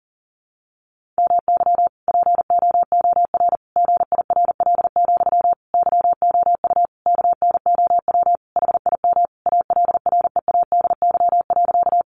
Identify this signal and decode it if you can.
Morse code, heard as a keyed tone, about 40 wpm, 700 Hz. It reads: MY POOR GIRL, YOU KNOW HIM ALREADY.
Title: Sentences from Top 1000 Words 40wpm